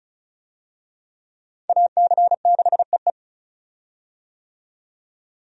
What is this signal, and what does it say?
Morse code, heard as a keyed tone, about 36 wpm, 700 Hz. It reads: AC6EE
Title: USA Call Signs - 5 Characters - 35wpm